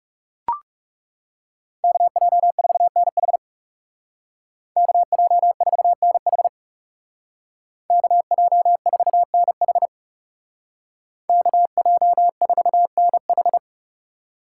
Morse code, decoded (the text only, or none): KJ4NH KJ4NH KJ4NH KJ4NH